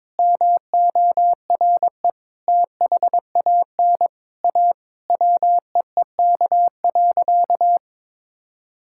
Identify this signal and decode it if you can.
Morse code, heard as a keyed tone, about 22 wpm, 700 Hz. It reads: MORE THAN A WEEK.
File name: practice / 22wpm / Sentences with 4-Word Phrases - 22wpm